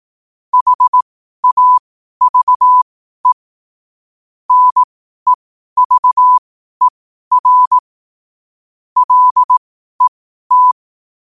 Morse code, heard as a keyed tone, about 18 wpm, spaced as for 12 wpm, 1000 Hz. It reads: HAVE NEVER LET
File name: qotdfiles / 12wpm